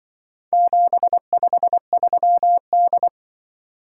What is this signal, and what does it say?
Morse code, heard as a keyed tone, about 23 wpm, 700 Hz. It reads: 753D
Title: Speed-Racing - Four Letter-Number Combinations - 20wpm